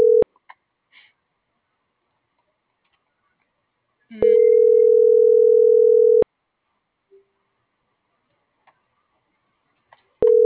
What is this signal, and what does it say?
Ringback tone (North America), 440 + 480 Hz.